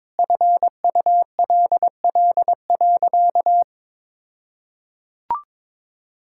Morse code, heard as a keyed tone, about 22 wpm, 700 Hz. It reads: FULL.
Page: morse-code-ninja-practice-22wpm